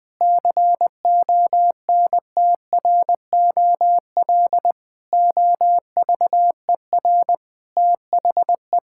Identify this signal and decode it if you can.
Morse code, heard as a keyed tone, about 20 wpm, 700 Hz. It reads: CONTROL OVER THE
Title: Sentences from Top 1000 Words - Review Part 5 - 20wpm